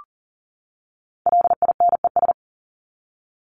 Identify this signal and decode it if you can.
Morse code, heard as a keyed tone, about 40 wpm, 700 Hz. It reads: LINES
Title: ES:S Suffix - Top 100 Words with Base Words 40wpm